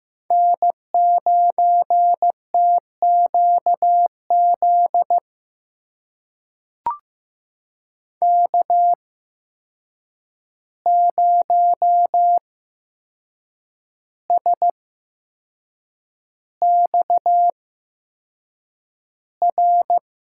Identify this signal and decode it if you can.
Morse code, heard as a keyed tone, about 15 wpm, spaced as for 6 wpm, 700 Hz. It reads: N9TQZ K0SXR
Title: USA Call Signs - 5 Characters - 8x Character Spacing - 15wpm